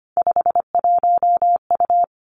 Morse code, heard as a keyed tone, about 25 wpm, 700 Hz. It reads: 51U
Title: Three Challenging Letter-Number Combinations 25wpm